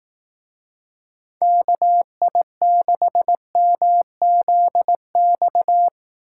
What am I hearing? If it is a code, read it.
Morse code, heard as a keyed tone, about 18 wpm, 700 Hz. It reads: KI6MZX
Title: Speed Racing - USA 2x3 Call Signs 15wpm